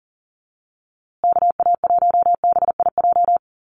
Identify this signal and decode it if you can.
Morse code, heard as a keyed tone, about 40 wpm, 700 Hz. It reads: KA1BIJ
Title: USA Call Signs Repeated 2x 40wpm